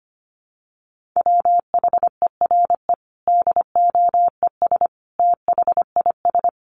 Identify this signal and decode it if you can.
Morse code, heard as a keyed tone, about 25 wpm, 700 Hz. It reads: WHERE DOES THIS
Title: Sentences from Top 1400 Words 25wpm